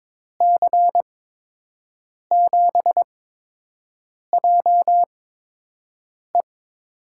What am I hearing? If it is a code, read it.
Morse code, heard as a keyed tone, about 22 wpm, spaced as for 6 wpm, 700 Hz. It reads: C7JE
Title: USA Call Signs - 8x Character Spacing - 22wpm